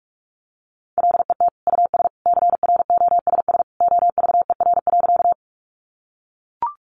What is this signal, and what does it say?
Morse code, heard as a keyed tone, about 44 wpm, 700 Hz. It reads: LET US CROSS OVER.
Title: Sentences from Top 300 Words 45wpm